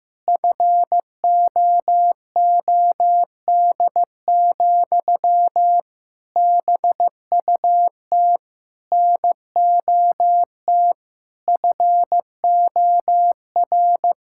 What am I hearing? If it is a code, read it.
Morse code, heard as a keyed tone, about 15 wpm, 700 Hz. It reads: FOOD, BUT NOT FOR
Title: Sentences from Top 1600 Words 15wpm